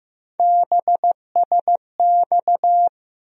Morse code, heard as a keyed tone, about 15 wpm, 700 Hz. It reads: BSX
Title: USA Call Sign Postfixes - 3 Letters 15wpm